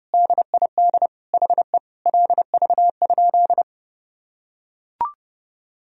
Morse code, heard as a keyed tone, about 32 wpm, 700 Hz. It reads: DID HE LV?
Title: Sentences from Top 400 Words and Abbreviations 30wpm